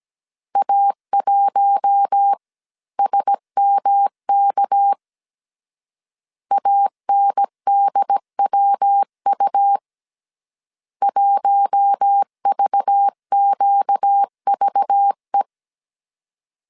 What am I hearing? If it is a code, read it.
Morse code, heard as a keyed tone, about 17 wpm, 800 Hz. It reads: A1SMK ANDWU 1VQVE
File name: CW / DET / 1 6 1500C S A 16wpm 2635